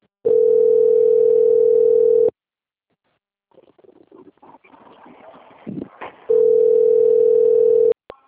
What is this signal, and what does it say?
Ringback tone (North America), 440 + 480 Hz.
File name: out- - - - - .726